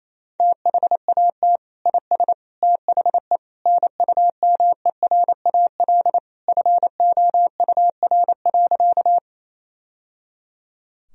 Morse code, heard as a keyed tone, about 28 wpm, 700 Hz. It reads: THAT IS THE NUMERAL FOUR.
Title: Sentences from Top 1000 Words - Review Part 2 - 28wpm